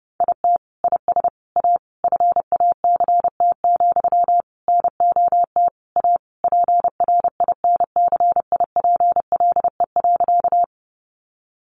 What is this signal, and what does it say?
Morse code, heard as a keyed tone, about 30 wpm, 700 Hz. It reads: IT IS A FACT, NOT A PRINCIPLE.
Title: Sentences from Top 1400 Words 30wpm